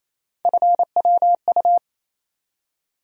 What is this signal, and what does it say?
Morse code, heard as a keyed tone, about 28 wpm, 700 Hz. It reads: FWU
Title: USA Call Sign Postfixes - 3 Letters 28wpm